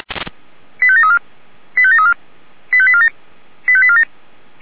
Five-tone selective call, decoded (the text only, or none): 85734 85734 85847 85847